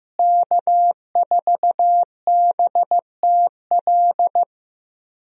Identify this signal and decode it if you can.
Morse code, heard as a keyed tone, about 16 wpm, 700 Hz. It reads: K4BTL
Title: USA Call Signs - 5 Characters - Repeated 3x - 15wpm